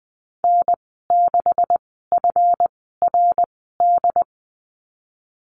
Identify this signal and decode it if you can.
Morse code, heard as a keyed tone, about 20 wpm, spaced as for 15 wpm, 700 Hz. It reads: N6FRD
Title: 140 Lesson 23g - Letter f - Call Signs 20wpm